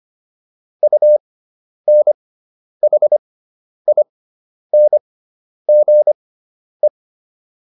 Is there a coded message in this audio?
Yes: Morse code from 0.8 to 6.9 s, heard as a keyed tone, about 25 wpm, spaced as for 10 wpm, 600 Hz. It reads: UNHINGE